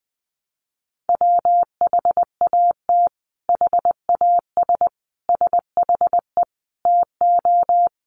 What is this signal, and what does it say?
Morse code, heard as a keyed tone, about 20 wpm, 700 Hz. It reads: WHAT HAS SHE TO
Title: Sentences from Top 100 Words 20wpm